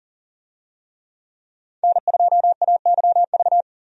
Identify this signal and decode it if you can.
Morse code, heard as a keyed tone, about 40 wpm, 700 Hz. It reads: N2AYV
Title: USA Call Signs - 5 Characters - 40wpm